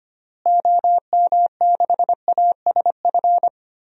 Morse code, heard as a keyed tone, about 25 wpm, 700 Hz. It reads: OM6ASF
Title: DXCC-100 Call Signs - 6 Characters - Repeated 3x - 25wpm